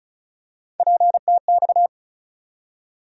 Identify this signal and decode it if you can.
Morse code, heard as a keyed tone, about 34 wpm, 700 Hz. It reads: PTX
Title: USA Call Sign Postfixes - 3 Letters 35wpm